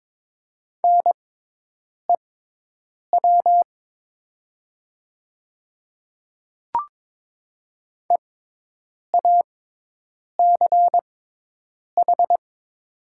Morse code, heard as a keyed tone, about 22 wpm, spaced as for 8 wpm, 700 Hz. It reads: NEW EACH
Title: 6x Character Spacing - Top 200 Words - 22wpm